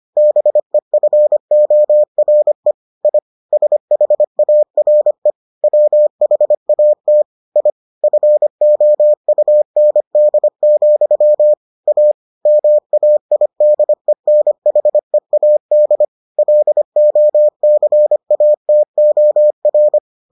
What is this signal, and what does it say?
Morse code, heard as a keyed tone, about 25 wpm, 600 Hz. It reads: BEFORE I SHARE WHAT I FOUND, A MAIDENHEAD LOCATOR